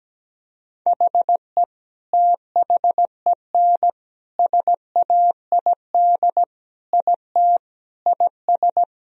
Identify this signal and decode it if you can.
Morse code, heard as a keyed tone, about 17 wpm, 700 Hz. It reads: HE THEN SAID IT IS
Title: Sentences from Top 1000 Words - Review Part 1 - 17wpm